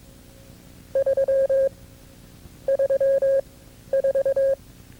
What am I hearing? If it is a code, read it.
Morse code, heard as a keyed tone, about 22 wpm, spaced as for 12 wpm, 540 Hz. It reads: 3 34
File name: CW-practice/KB6MT-tape2-side1